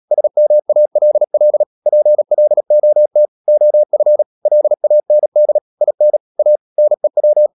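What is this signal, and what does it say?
Morse code, heard as a keyed tone, about 37 wpm, 600 Hz. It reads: SMALL PLOT OF LAND IN A NEW
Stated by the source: Morse code practice at 35–39 wpm sending English text of Ken Follett's novel Never, from CW med Gnister.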